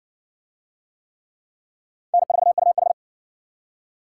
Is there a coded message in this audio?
Yes: Morse code from 2.1 to 2.9 s, heard as a keyed tone, about 53 wpm, 700 Hz. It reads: N4UH